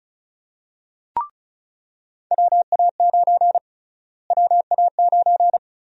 Morse code, heard as a keyed tone, about 34 wpm, 700 Hz. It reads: WA9 WA9
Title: USA Call Sign Prefixes - Repeated 2x - 35wpm